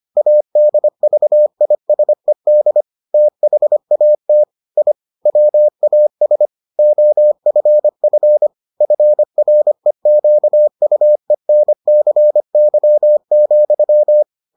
Morse code, heard as a keyed tone, about 25 wpm, 600 Hz. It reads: ADVISED THAT I WAS OFF FREQUENCY,